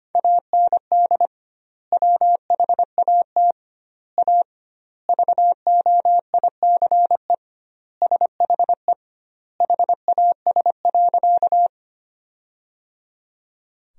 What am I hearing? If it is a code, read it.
Morse code, heard as a keyed tone, about 25 wpm, 700 Hz. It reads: AND WHAT A VOICE SHE HAS.
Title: 2x Word-Spacing - Sentences from Top 500 Words - 25wpm